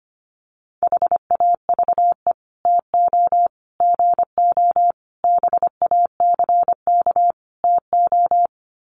Morse code, heard as a keyed tone, about 25 wpm, 700 Hz. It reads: HAVE TO GO BACK TO
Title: Sets of 5 Words Encore2 25wpm